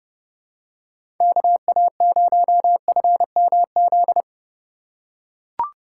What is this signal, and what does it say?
Morse code, heard as a keyed tone, about 30 wpm, 700 Hz. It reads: KA0FMZ
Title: USA Call Signs - 6 Characters - 2x Character Spacing - 30wpm